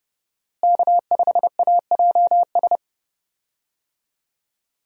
Morse code, heard as a keyed tone, about 31 wpm, 700 Hz. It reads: K5AJS
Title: USA Call Signs - 5 Characters - Repeated 3x - 30wpm